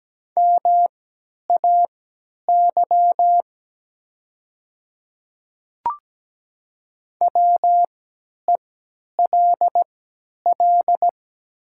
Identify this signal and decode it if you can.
Morse code, heard as a keyed tone, about 17 wpm, spaced as for 10 wpm, 700 Hz. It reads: MAY WELL
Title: 3x Character Spacing - Top 200 Words - 17wpm